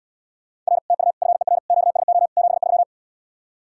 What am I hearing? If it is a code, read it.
Morse code, heard as a keyed tone, about 77 wpm, 700 Hz. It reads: R TU FER QSO 73